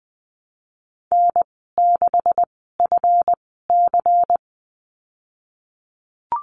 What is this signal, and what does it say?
Morse code, heard as a keyed tone, about 20 wpm, spaced as for 15 wpm, 700 Hz. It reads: N6FC
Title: 167 Lesson 27g - Number 7 - Call Signs 20wpm